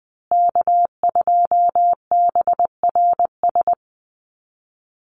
Morse code, heard as a keyed tone, about 20 wpm, 700 Hz. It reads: K2BRS
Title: USA Call Signs Repeated 2x 20wpm